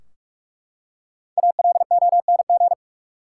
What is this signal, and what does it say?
Morse code, heard as a keyed tone, about 44 wpm, 700 Hz. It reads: ALONG
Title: Five Letter Words - 45wpm